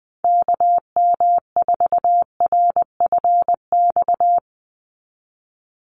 Morse code, heard as a keyed tone, about 20 wpm, 700 Hz. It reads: KM4RFX
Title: USA Call Signs 20wpm